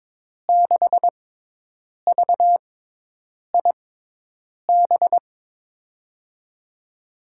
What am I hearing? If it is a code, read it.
Morse code, heard as a keyed tone, about 21 wpm, spaced as for 8 wpm, 700 Hz. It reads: 6VIB
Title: USA Call Signs - 6x Character Spacing - 22wpm